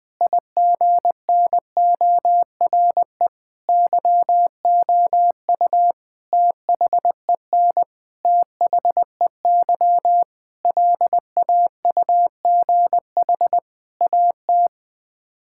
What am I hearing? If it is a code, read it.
Morse code, heard as a keyed tone, about 21 wpm, 700 Hz. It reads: IGNORE YOU THEN THEY LAUGH AT
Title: Sets of 8 Words - 20wpm